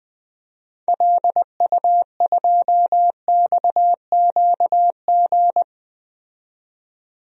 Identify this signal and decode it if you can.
Morse code, heard as a keyed tone, about 20 wpm, 700 Hz. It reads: LU2XQG
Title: DXCC-100 Call Signs - 6 Characters - 20wpm